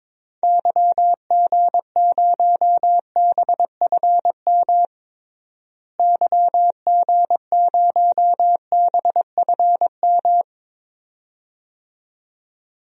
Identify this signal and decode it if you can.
Morse code, heard as a keyed tone, about 22 wpm, 700 Hz. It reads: YG0BFM YG0BFM
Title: DXCC-100 Call Signs - 6 Characters - Repeated 2x - 22wpm